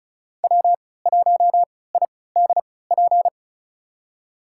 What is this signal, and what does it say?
Morse code, heard as a keyed tone, about 36 wpm, spaced as for 20 wpm, 700 Hz. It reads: W1IDP
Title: USA Call Signs - 5 Characters - 3x Character Spacing - 35wpm